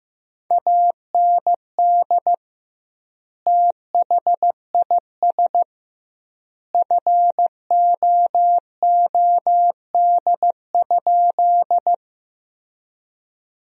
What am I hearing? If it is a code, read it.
Morse code, heard as a keyed tone, about 15 wpm, 700 Hz. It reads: AND THIS FOOD?
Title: 2x Word-Spacing - Sentences from Top 300 Words - 15wpm